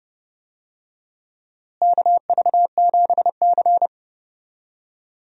Morse code, heard as a keyed tone, about 30 wpm, 700 Hz. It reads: KV7C
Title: USA Call Signs - 4 Characters - Repeated 3x - 30wpm